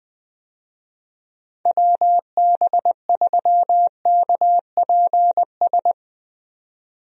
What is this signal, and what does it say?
Morse code, heard as a keyed tone, about 21 wpm, 700 Hz. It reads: WB3KPS
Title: USA Call Signs - 6 Characters - Repeated 2x - 20wpm